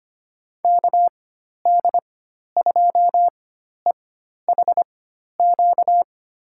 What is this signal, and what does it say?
Morse code, heard as a keyed tone, about 25 wpm, spaced as for 12 wpm, 700 Hz. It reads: KD2EHQ
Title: USA Call Signs - 6 Characters - 4x Character Spacing - 25wpm